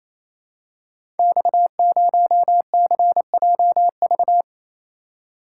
Morse code, heard as a keyed tone, about 27 wpm, 700 Hz. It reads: X0CJV
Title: Code Groups - 5 Letter:Numbers - 28wpm